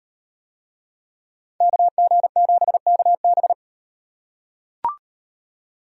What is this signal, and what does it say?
Morse code, heard as a keyed tone, about 38 wpm, 700 Hz. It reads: KG7KB